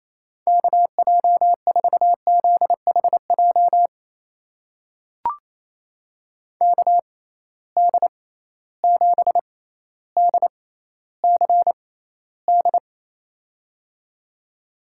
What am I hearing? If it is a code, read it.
Morse code, heard as a keyed tone, about 28 wpm, spaced as for 14 wpm, 700 Hz. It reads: KJ4ZHJ KD7DCD
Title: USA Call Signs - 6 Characters - 6x Character Spacing - 28wpm